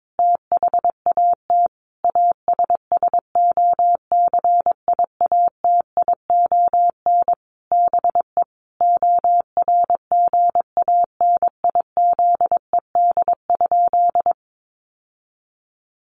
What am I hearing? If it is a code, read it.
Morse code, heard as a keyed tone, about 22 wpm, 700 Hz. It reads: THAT ASSOCIATION BE ORGANIZED?
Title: Sentences from Top 1300 Words 22wpm